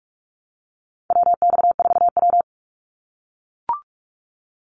Morse code, heard as a keyed tone, about 44 wpm, 700 Hz. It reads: WX4W
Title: Call Signs - 45wpm